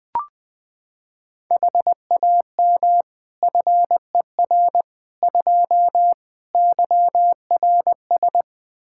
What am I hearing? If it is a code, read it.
Morse code, heard as a keyed tone, about 20 wpm, 700 Hz. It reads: HAM FER 2 YRS
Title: Practice QSO - Exchange 3 - More Info - 20wpm